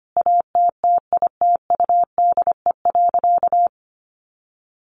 Morse code, heard as a keyed tone, about 25 wpm, 700 Hz. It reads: ATTITUDE.
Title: Sentences from Top 1300 Words 25wpm